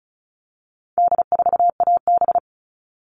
Morse code, heard as a keyed tone, about 34 wpm, 700 Hz. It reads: D4AB
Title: All Cut Numbers - 4-Digit 35wpm